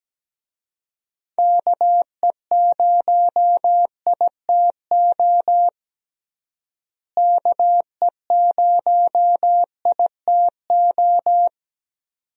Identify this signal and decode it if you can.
Morse code, heard as a keyed tone, about 17 wpm, 700 Hz. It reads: KE0ITO KE0ITO
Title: USA Call Signs - 6 Characters - Repeated 2x - 17wpm